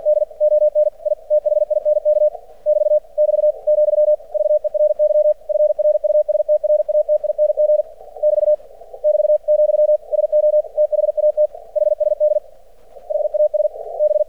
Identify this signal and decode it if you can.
Morse code, heard as a keyed tone, about 48 wpm, 600 Hz. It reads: D ON I THINK <BT> <BT> , VERY FRUSTRATING <BT> <BT> , SO THAT HSD HAS B
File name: archived / CRSnet